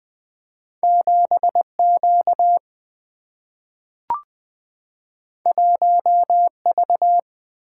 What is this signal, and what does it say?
Morse code, heard as a keyed tone, about 20 wpm, 700 Hz. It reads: 7Q 1V
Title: Single-Letter Single-Number 20wpm